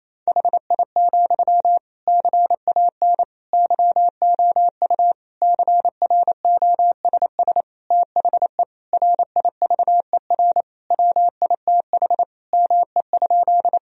Morse code, heard as a keyed tone, about 28 wpm, 700 Hz. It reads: HI, CAN YOU CROSS THE RIVER WITH ME?
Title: Sentences from Top 1000 Words - Review Part 3 - 28wpm